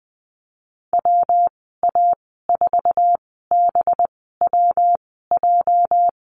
Morse code, heard as a keyed tone, about 20 wpm, spaced as for 14 wpm, 700 Hz. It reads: WA4BWJ
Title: 231 Lesson 37g - Letter x - Call Signs 20wpm